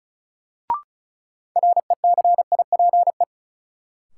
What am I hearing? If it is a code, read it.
Morse code, heard as a keyed tone, about 34 wpm, 700 Hz. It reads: RECIPE
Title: Sentences from Top 1200 Words - New Word Drill - 35wpm